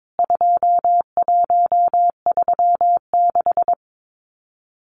Morse code, heard as a keyed tone, about 22 wpm, 700 Hz. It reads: 2136
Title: CWT Exchange 22wpm